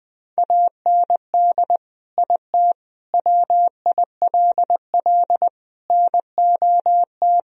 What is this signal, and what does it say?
Morse code, heard as a keyed tone, about 20 wpm, 700 Hz. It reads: AND IT WILL NOT
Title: Sentences from Top 1000 Words - Review Part 1 - 20wpm